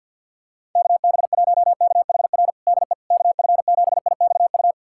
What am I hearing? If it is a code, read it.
Morse code, heard as a keyed tone, about 51 wpm, 700 Hz. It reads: KB1KHR DE KF7IXU
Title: USA Callsign DE Callsign - 6 Characters - 50wpm